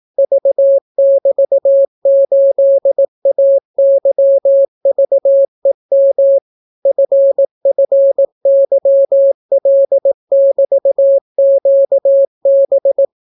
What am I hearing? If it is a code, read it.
Morse code, heard as a keyed tone, about 18 wpm, 550 Hz. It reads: V<BT>8AYVEM FFYL<BT>QB